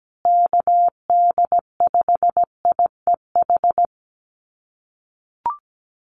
Morse code, heard as a keyed tone, about 17 wpm, 700 Hz. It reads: KD5IEH
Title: USA Call Signs Repeated 2x 17wpm